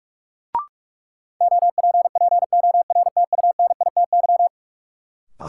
Morse code, heard as a keyed tone, about 42 wpm, 700 Hz. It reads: OPPORTUNITY
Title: Sentences from Top 1000 Words - New Word Drill - 45wpm